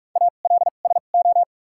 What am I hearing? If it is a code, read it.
Morse code, heard as a keyed tone, about 45 wpm, spaced as for 32 wpm, 700 Hz. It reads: ALSO
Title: 2x Character Spacing - Top 200 Words - 45wpm